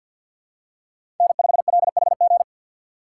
Morse code, heard as a keyed tone, about 51 wpm, 700 Hz. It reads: N5LHG
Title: USA Call Signs - 5 Characters - 4x Character Spacing - 50wpm